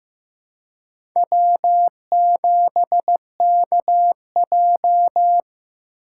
Morse code, heard as a keyed tone, about 15 wpm, 700 Hz. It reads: W7KJ
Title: USA Call Signs - 2x Character Spacing - 15wpm